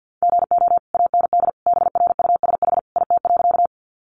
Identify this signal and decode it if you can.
Morse code, heard as a keyed tone, about 48 wpm, 700 Hz. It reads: GO AND BRUSH IT.
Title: Sentences from Top 1100 Words 50wpm